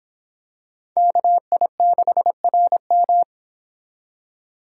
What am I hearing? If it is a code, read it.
Morse code, heard as a keyed tone, about 26 wpm, 700 Hz. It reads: KI6RM